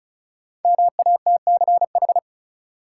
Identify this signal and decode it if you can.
Morse code, heard as a keyed tone, about 36 wpm, 700 Hz. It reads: MATCH
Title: Sentences from Top 1600 Words - New Word Drill - 35wpm